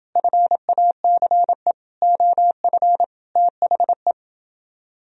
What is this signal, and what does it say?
Morse code, heard as a keyed tone, about 29 wpm, 700 Hz. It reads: FACE OF THE